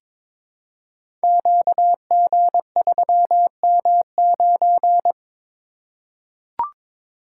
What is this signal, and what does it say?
Morse code, heard as a keyed tone, about 22 wpm, 700 Hz. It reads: QG3M9
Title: Code Groups - 5 Letter:Numbers - 22wpm